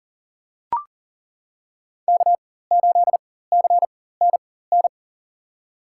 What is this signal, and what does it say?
Morse code, heard as a keyed tone, about 40 wpm, spaced as for 19 wpm, 700 Hz. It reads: K8CNN
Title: USA Call Signs - 5 Characters - 4x Character Spacing - 40wpm